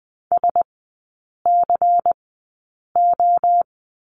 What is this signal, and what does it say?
Morse code, heard as a keyed tone, about 20 wpm, spaced as for 8 wpm, 700 Hz. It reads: SCO
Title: 107 Lesson 18e - Letter m - Three Characters 20wpm